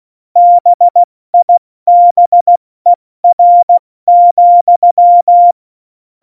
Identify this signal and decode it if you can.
Morse code, heard as a keyed tone, about 16 wpm, spaced as for 14 wpm, 700 Hz. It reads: BIBER,
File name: Morsekurs / Witze / Witz 08. Dezember 2020, 70 F80